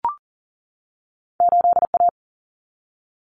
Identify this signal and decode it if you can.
Morse code, heard as a keyed tone, about 40 wpm, 700 Hz. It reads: 8A
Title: Common Cut Numbers - 2-Digit 40wpm